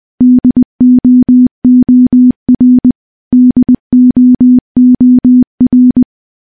Morse code, heard as a keyed tone, about 20 wpm, 250 Hz. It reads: DOOR DOOR
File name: morse / morse-training-20WPM